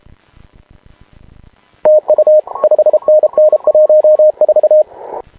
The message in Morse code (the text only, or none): TU 5NN 14